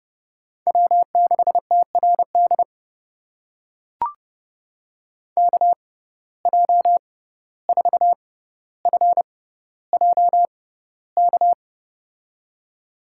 Morse code, heard as a keyed tone, about 30 wpm, spaced as for 15 wpm, 700 Hz. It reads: W6TRD KJ4FJK
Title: USA Call Signs - 6x Character Spacing - 30wpm